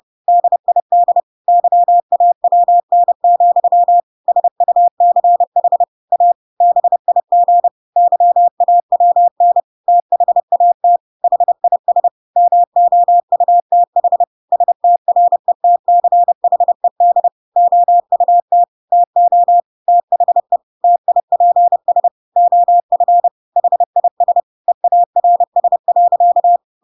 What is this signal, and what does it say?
Morse code, heard as a keyed tone, about 30 wpm, 700 Hz. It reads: DID YAWN, SUCH A BIG YAWN THAT HIS MOUTH STRETCHED OUT TO THE TIPS OF HIS EARS.
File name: Pinocchio / Pinocchio ch5 30 30 ik1hge